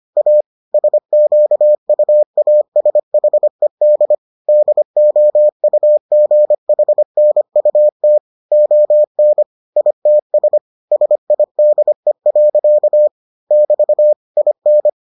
Morse code, heard as a keyed tone, about 25 wpm, 600 Hz. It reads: A SQUASHED DOUGHNUT ON ITS SIDE. <BT> IN